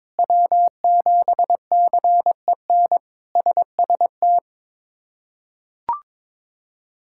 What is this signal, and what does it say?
Morse code, heard as a keyed tone, about 22 wpm, 700 Hz. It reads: W7CEN SST
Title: K1USN SST QSO Practice - 22wpm